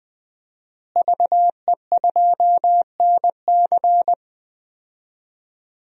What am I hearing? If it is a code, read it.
Morse code, heard as a keyed tone, about 20 wpm, 700 Hz. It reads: VE2NC
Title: DXCC-100 Call Signs - 5 Characters - Repeated 2x - 20wpm